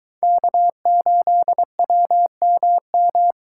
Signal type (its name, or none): Morse code